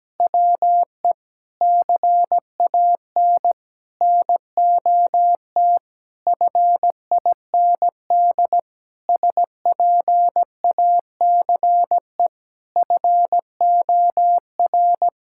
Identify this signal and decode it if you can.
Morse code, heard as a keyed tone, about 17 wpm, 700 Hz. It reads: WE CAN NOT FIND SPACE FOR
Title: Sentences from Top 1000 Words - Review Part 5 - 17wpm